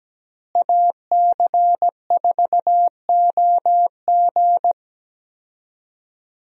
Morse code, heard as a keyed tone, about 17 wpm, 700 Hz. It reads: AC4OG
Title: USA Call Signs - 5 Characters - Repeated 3x - 17wpm